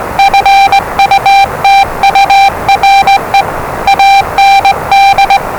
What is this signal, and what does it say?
Morse code, heard as a keyed tone, about 20 wpm, 790 Hz, with noise. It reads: FUTURE AND
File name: WellingtonRadioZlwCloseDownOnHfAndMf / 01-Zlw 4277khz 417.5 qrt until 0925z